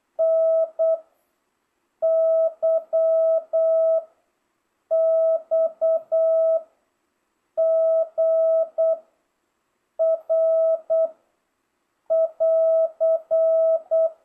Morse code, heard as a keyed tone, about 8 wpm, 650 Hz. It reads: N Y X G R <AR>